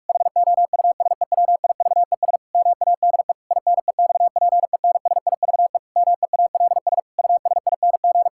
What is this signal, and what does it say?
Morse code, heard as a keyed tone, about 41 wpm, 700 Hz. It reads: HOUSEWIVES MADE INEXPENSIVE MEALS USING
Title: Pangram Challenge v2 - 45wpm